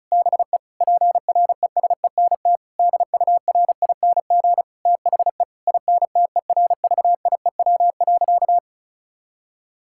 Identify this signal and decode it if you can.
Morse code, heard as a keyed tone, about 34 wpm, 700 Hz. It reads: BE PRESENT DURING THE INTERVIEW.
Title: Sentences from Top 1600 Words 35wpm